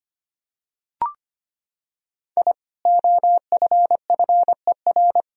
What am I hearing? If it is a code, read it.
Morse code, heard as a keyed tone, about 25 wpm, 700 Hz. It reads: I OFFER
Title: Sentences from Top 1000 Words - Review Part 4 - 25wpm